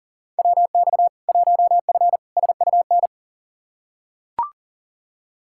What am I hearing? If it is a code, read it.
Morse code, heard as a keyed tone, about 37 wpm, 700 Hz. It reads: WX 1F SUN
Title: Practice QSO - Exchange 2 - Info - 40wpm